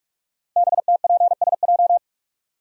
Morse code, heard as a keyed tone, about 44 wpm, 700 Hz. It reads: BTPSJ
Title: Code Groups - 5 Letters - 45wpm